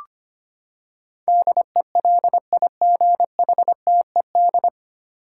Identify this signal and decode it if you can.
Morse code, heard as a keyed tone, about 25 wpm, 700 Hz. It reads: DELIGHTED
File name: Nine Letter Words - 25wpm